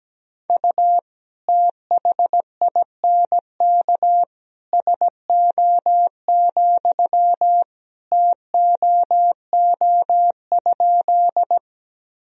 Morse code, heard as a keyed tone, about 17 wpm, 700 Hz. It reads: U THINK SO, TOO?
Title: Sentences from Top 200 Words and Abbreviations 17wpm